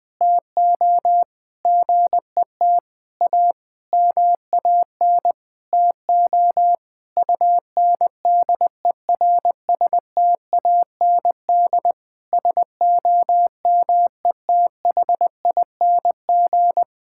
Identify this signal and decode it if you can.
Morse code, heard as a keyed tone, about 20 wpm, 700 Hz. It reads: TO GET A MAN TO UNDERSTAND SOMETHING
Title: Sentences with 9-Word Phrases - 20wpm